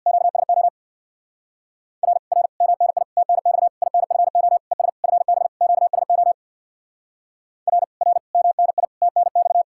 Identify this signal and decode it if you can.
Morse code, heard as a keyed tone, about 64 wpm, 700 Hz. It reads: QSO R R MNI TNX INFO ES FB QSO R R MNI TNX